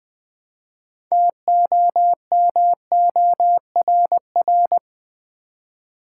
Morse code, heard as a keyed tone, about 20 wpm, 700 Hz. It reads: TOMORR
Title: Build A Word - 2 Letters At A Time - 8-Letter Words - 20wpm